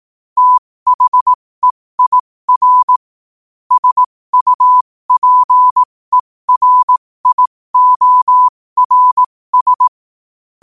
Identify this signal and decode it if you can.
Morse code, heard as a keyed tone, about 18 wpm, spaced as for 15 wpm, 1000 Hz. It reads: THEIR SUPERIORS
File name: qotdfiles / 15wpm